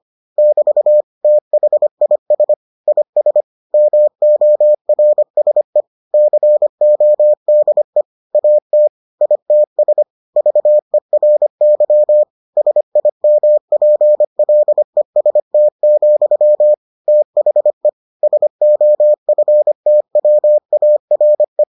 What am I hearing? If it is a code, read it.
Morse code, heard as a keyed tone, about 25 wpm, 600 Hz. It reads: <BT> THIS IS MORSE CODE AT ITS VERY SIMPLEST, THE SOFTWARE